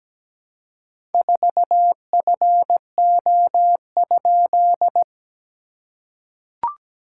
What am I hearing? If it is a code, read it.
Morse code, heard as a keyed tone, about 17 wpm, 690 Hz. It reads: 4FO?
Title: SOTA QSO Practice 17wpm